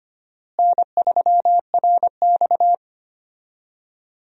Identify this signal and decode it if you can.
Morse code, heard as a keyed tone, about 25 wpm, 700 Hz. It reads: N3RX
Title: USA Call Signs - 4 Characters - Repeated 3x - 25wpm